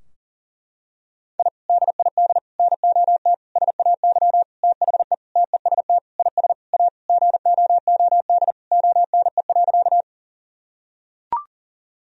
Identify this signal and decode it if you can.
Morse code, heard as a keyed tone, about 40 wpm, 700 Hz. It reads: I DID NOT SAY THE TEST IS A GOOD ONE.
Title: Sentences from Top 1000 Words - Review Part 3 - 40wpm